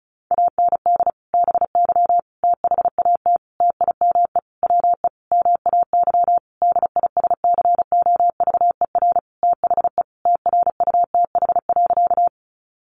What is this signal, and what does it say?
Morse code, heard as a keyed tone, about 34 wpm, 700 Hz. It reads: AND BY THAT TIME WE MAY DISCOVER THE TRUTH.
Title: Sentences from Top 1100 Words 35wpm